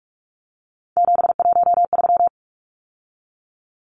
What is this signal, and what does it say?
Morse code, heard as a keyed tone, about 44 wpm, 700 Hz. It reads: 713
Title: Numbers - 3-Digit 45wpm